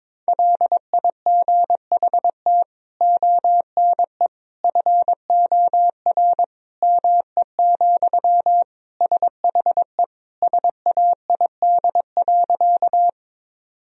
Morse code, heard as a keyed tone, about 22 wpm, 700 Hz. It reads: LIGHT ONE FOR ME, SHE SAID.
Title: Sentences from Top 1000 Words - Review Part 1 - 22wpm